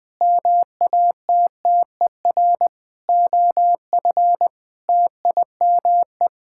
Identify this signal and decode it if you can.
Morse code, heard as a keyed tone, about 19 wpm, 700 Hz. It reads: MATTER OF TIME
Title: Sets of 9 Words - 20wpm